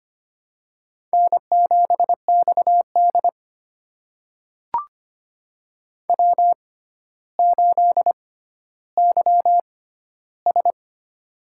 Morse code, heard as a keyed tone, about 25 wpm, spaced as for 13 wpm, 700 Hz. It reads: N7XD W8YS
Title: USA Call Signs - 4 Characters - 6x Character Spacing - 25wpm